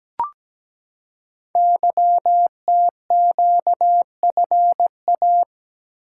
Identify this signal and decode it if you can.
Morse code, heard as a keyed tone, about 17 wpm, 700 Hz. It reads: YTQFA
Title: Code Groups - 5 Letters - 17wpm